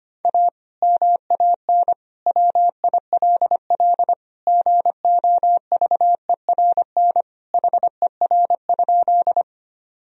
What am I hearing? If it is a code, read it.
Morse code, heard as a keyed tone, about 25 wpm, 700 Hz. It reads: A MAN WILL GOVERN HER?
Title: Sentences from Top 1000 Words - Review Part 2 - 25wpm